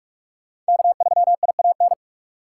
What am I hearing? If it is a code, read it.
Morse code, heard as a keyed tone, about 57 wpm, 690 Hz. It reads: K3IAN